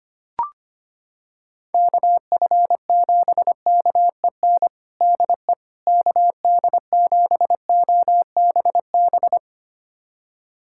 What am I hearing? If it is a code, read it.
Morse code, heard as a keyed tone, about 25 wpm, 700 Hz. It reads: KF7KEN DE KD7OBB
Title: USA Callsign DE Callsign - 6 Characters - 25wpm